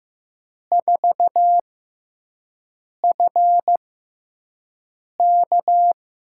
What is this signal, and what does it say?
Morse code, heard as a keyed tone, about 15 wpm, spaced as for 5 wpm, 700 Hz. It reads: 4FK
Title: USA Call Signs - 6 Characters - 6x Character Spacing - 15wpm